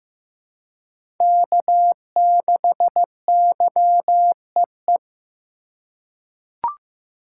Morse code, heard as a keyed tone, about 16 wpm, 700 Hz. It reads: K6YEE
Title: USA Call Signs - 2x Character Spacing - 15wpm